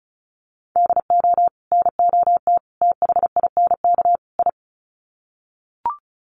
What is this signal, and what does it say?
Morse code, heard as a keyed tone, about 34 wpm, 700 Hz. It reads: DO NOT THINK I
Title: Sets of 4 Words 35wpm